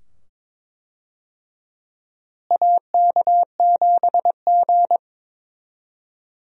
Morse code, heard as a keyed tone, about 22 wpm, 700 Hz. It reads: AK7G
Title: USA Call Signs - 4 Characters - Repeated 2x - 22wpm